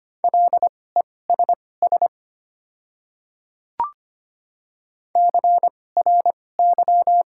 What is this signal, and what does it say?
Morse code, heard as a keyed tone, about 25 wpm, spaced as for 18 wpm, 690 Hz. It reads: LESS CRY
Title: 2x Character Spacing - Top 500 Words - 25wpm